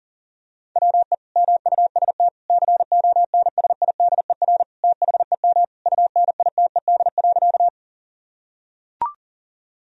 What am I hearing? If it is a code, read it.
Morse code, heard as a keyed tone, about 40 wpm, 700 Hz. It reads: WE MUST CONSIDER THEM UNITED.
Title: Sentences from Top 1000 Words - Review Part 4 - 40wpm